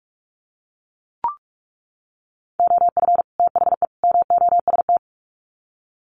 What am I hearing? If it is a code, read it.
Morse code, heard as a keyed tone, about 44 wpm, 700 Hz. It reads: OF THE MOST
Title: Sets of 3 Words 45wpm